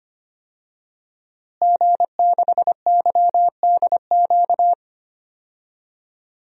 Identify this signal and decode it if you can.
Morse code, heard as a keyed tone, about 25 wpm, 700 Hz. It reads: G6YDQ
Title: DXCC-100 Call Signs - 5 Characters - Repeated 2x - 25wpm